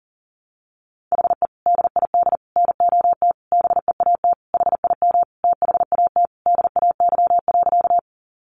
Morse code, heard as a keyed tone, about 40 wpm, 700 Hz. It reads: HE DID NOT BEAT HIM THAT DAY.
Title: Sentences from Top 1300 Words 40wpm